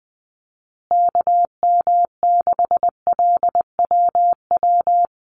Morse code, heard as a keyed tone, about 20 wpm, 700 Hz. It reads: KM6LWW
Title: USA Call Signs Repeated 2x 20wpm